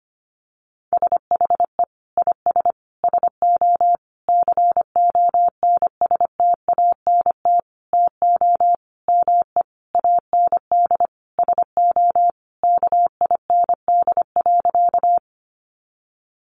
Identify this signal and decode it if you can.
Morse code, heard as a keyed tone, about 25 wpm, 700 Hz. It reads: SHE IS SO CONSTANT TO ME AND SO KIND.
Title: Sentences from Top 1500 Words 25wpm